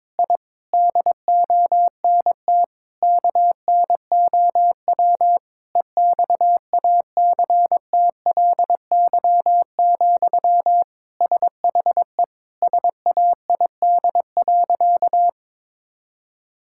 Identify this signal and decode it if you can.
Morse code, heard as a keyed tone, about 22 wpm, 700 Hz. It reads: I DONT KNOW EXACTLY, SHE SAID.
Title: Sentences from Top 1000 Words - Review Part 6 - 22wpm